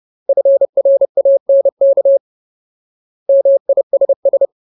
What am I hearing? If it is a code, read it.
Morse code, heard as a keyed tone, about 30 wpm, 550 Hz. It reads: FRANK MISS
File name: Ultimate Word List / 30WPM Ultimate